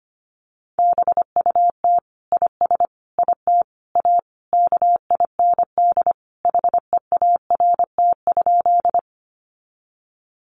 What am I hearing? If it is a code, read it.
Morse code, heard as a keyed tone, about 25 wpm, 700 Hz. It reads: BUT IS IT A KIND HEART?
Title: Sentences from Top 1200 Words 25wpm